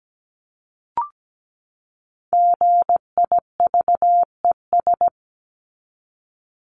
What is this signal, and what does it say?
Morse code, heard as a keyed tone, about 17 wpm, 700 Hz. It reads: GIVES
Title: ES:S Suffix - Top 100 Words with Base Words 17wpm